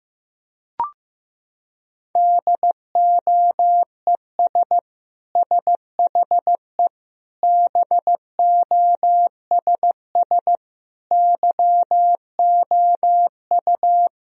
Morse code, heard as a keyed tone, about 15 wpm, 700 Hz. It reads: DOES SHE BOSS YOU
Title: Sentences from Top 1000 Words - Review Part 4 - 15wpm